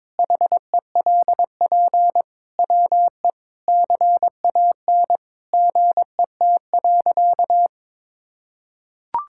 Morse code, heard as a keyed tone, about 22 wpm, 700 Hz. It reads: HELP WE CAN GET.
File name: practice / 22wpm / Sentences with 8-Word Phrases - 22wpm